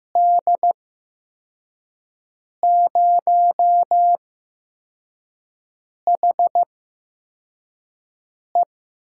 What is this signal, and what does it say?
Morse code, heard as a keyed tone, about 15 wpm, spaced as for 4 wpm, 700 Hz. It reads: D0HE
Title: USA Call Signs - 6 Characters - 8x Character Spacing - 15wpm